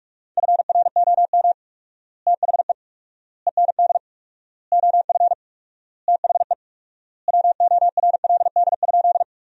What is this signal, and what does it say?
Morse code, heard as a keyed tone, about 41 wpm, 700 Hz. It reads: FROM THE END OF THE WORLD?
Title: 4x Word-Spacing - Sentences from Top 200 Words - 45wpm